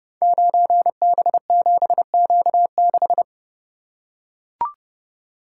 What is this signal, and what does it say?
Morse code, heard as a keyed tone, about 30 wpm, 700 Hz. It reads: 9B7Q6
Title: Code Groups - 5 Letter:Numbers - 30wpm